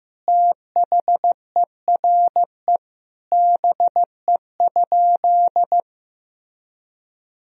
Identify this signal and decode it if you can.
Morse code, heard as a keyed tone, about 15 wpm, 700 Hz. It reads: THERE BE?
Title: Sentences from Top 1000 Words - Review Part 3 - 15wpm